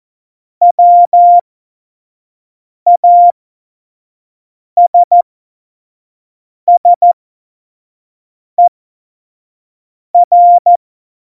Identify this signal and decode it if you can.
Morse code, heard as a keyed tone, about 14 wpm, spaced as for 5 wpm, 700 Hz. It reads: WASSER
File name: Morsekurs / Witze / Witz 12. September 17, Tempo 25